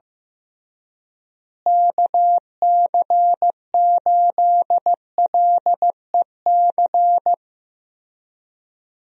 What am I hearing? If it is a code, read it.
Morse code, heard as a keyed tone, about 15 wpm, 700 Hz. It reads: KC8LEC
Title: USA Call Signs - 6 Characters - Repeated 3x - 15wpm